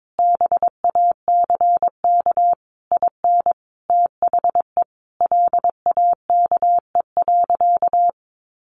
Morse code, heard as a keyed tone, about 22 wpm, 700 Hz. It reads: BACK IN THE LAKE.
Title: Sentences from Top 1100 Words 22wpm